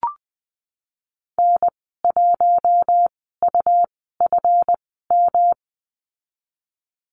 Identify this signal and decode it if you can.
Morse code, heard as a keyed tone, about 20 wpm, spaced as for 14 wpm, 700 Hz. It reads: N1UFM
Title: 173 Lesson 28g - Number 9 - Call Signs 20wpm